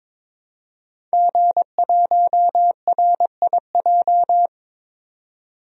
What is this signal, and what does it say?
Morse code, heard as a keyed tone, about 22 wpm, 700 Hz. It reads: G1RIJ
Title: DXCC-100 Call Signs - 5 Characters - 22wpm